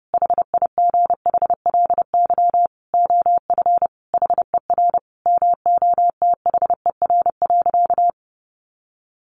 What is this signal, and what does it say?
Morse code, heard as a keyed tone, about 30 wpm, 700 Hz. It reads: HIGHLY OF HER MOTHER.
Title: Sentences from Top 1100 Words 30wpm